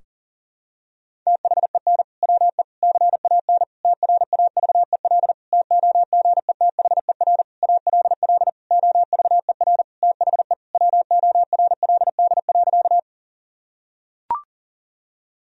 Morse code, heard as a keyed tone, about 37 wpm, 700 Hz. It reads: THEN WE CAN TRAVEL TOGETHER ALL OVER THE WORLD.
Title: Sentences from Top 1000 Words - Review Part 2 - 40wpm